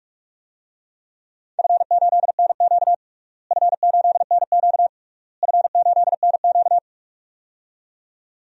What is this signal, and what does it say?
Morse code, heard as a keyed tone, about 42 wpm, 700 Hz. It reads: F8NQ F8NQ F8NQ
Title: DXCC-100 Call Signs - 4 Characters - Repeated 3x - 45wpm